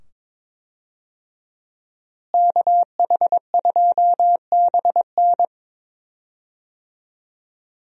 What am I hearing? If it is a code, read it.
Morse code, heard as a keyed tone, about 22 wpm, 700 Hz. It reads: KH2BN
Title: USA Call Signs - 5 Characters - Repeated 2x - 22wpm